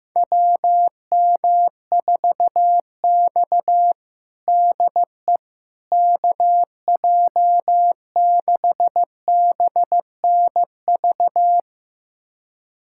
Morse code, heard as a keyed tone, about 15 wpm, 700 Hz. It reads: WM4X DE KJ6BNV
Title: USA Callsign DE Callsign 15wpm